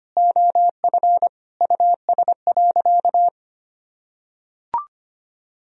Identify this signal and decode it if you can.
Morse code, heard as a keyed tone, about 25 wpm, 700 Hz. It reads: OF US.
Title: Sentences from Top 1000 Words - Review Part 2 - 25wpm